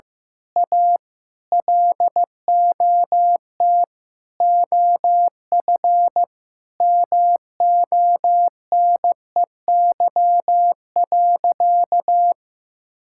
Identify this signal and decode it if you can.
Morse code, heard as a keyed tone, about 15 wpm, 700 Hz. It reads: A LOT OF MONEY.
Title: Sentences from Top 1000 Words - Review Part 6 - 15wpm